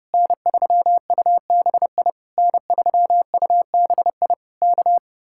Morse code, heard as a keyed tone, about 33 wpm, 700 Hz. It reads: N3UBI N3UBI K